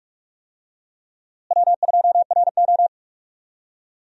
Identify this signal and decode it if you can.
Morse code, heard as a keyed tone, about 46 wpm, 700 Hz. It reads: W2RO
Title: USA Call Signs - 4 Characters - Repeated 3x - 45wpm